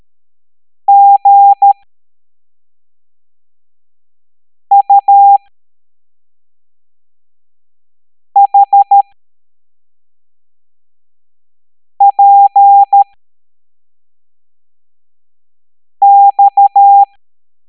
Morse code, heard as a keyed tone, about 13 wpm, spaced as for 3 wpm, 800 Hz. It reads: GUHPX
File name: hamdata / L07-EX02-HQP